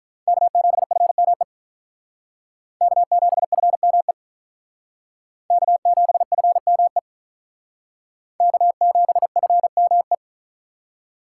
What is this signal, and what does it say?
Morse code, heard as a keyed tone, about 45 wpm, 700 Hz. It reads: K7FME K7FME K7FME K7FME